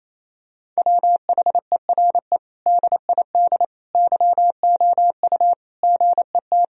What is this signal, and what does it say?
Morse code, heard as a keyed tone, about 27 wpm, 700 Hz. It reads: WHERE DID YOU GET
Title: Sentences from Top 1000 Words - Review Part 3 - 28wpm